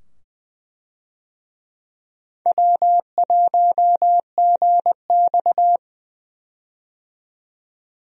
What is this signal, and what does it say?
Morse code, heard as a keyed tone, about 20 wpm, 700 Hz. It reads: W1GX
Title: USA Call Signs - 4 Characters - 20wpm